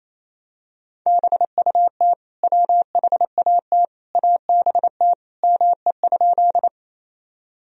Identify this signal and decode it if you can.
Morse code, heard as a keyed tone, about 28 wpm, 700 Hz. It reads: BUT WHAT ABT ME?
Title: Sentences from Top 200 Words and Abbreviations 28wpm